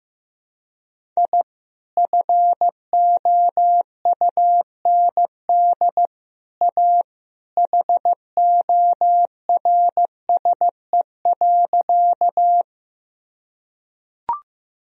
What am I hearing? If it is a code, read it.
Morse code, heard as a keyed tone, about 15 wpm, 700 Hz. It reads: I FOUND A HORSE.
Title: Sentences from Top 1000 Words - Review Part 1 - 15wpm